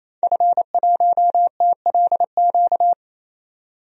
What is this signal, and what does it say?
Morse code, heard as a keyed tone, about 28 wpm, 700 Hz. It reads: F1TLQ
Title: DXCC-100 Call Signs - 5 Characters - 28wpm